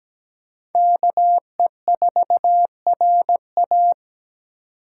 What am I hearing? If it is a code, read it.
Morse code, heard as a keyed tone, about 17 wpm, 700 Hz. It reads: KE4RA
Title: USA Call Signs - 5 Characters - 3x Character Spacing - 17wpm